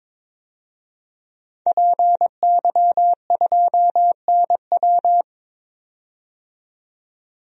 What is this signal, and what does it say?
Morse code, heard as a keyed tone, about 22 wpm, 700 Hz. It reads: PY2NW
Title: DXCC-100 Call Signs - 5 Characters - 22wpm